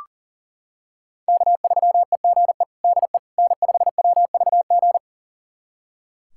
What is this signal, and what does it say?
Morse code, heard as a keyed tone, about 37 wpm, 700 Hz. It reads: K3EGE DE N5WVG
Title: USA Callsign DE Callsign - 5 Characters - 40wpm